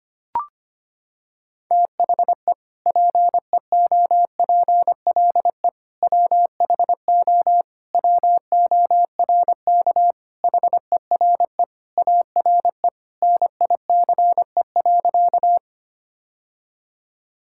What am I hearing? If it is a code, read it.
Morse code, heard as a keyed tone, about 25 wpm, 700 Hz. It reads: THE PEOPLE WHO WORK HERE ARE NICE.
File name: practice / 25wpm / Sentences with 3-Word Phrases - 25wpm